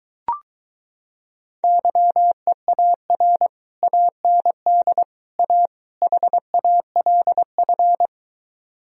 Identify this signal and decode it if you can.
Morse code, heard as a keyed tone, about 23 wpm, 700 Hz. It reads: YEAR AND A HALF